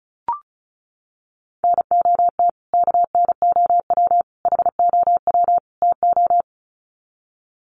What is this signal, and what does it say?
Morse code, heard as a keyed tone, about 34 wpm, 700 Hz. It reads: NOT KNOW HOW TO
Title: Sets of 4 Words 35wpm